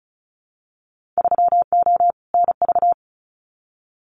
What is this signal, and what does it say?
Morse code, heard as a keyed tone, about 34 wpm, 700 Hz. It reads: 3O NV
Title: Winter Field Day Exchanges Encore 35wpm